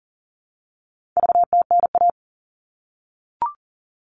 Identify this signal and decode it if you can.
Morse code, heard as a keyed tone, about 40 wpm, 700 Hz. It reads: VTNA
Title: All Cut Numbers - 4-Digit 40wpm